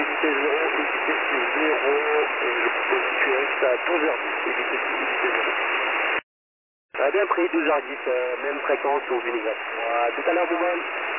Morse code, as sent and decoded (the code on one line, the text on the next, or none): none